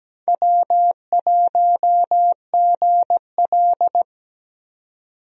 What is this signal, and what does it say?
Morse code, heard as a keyed tone, about 17 wpm, 700 Hz. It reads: W1GL
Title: USA Call Signs - 4 Characters - 17wpm